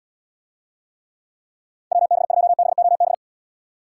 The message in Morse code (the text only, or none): WB2BOB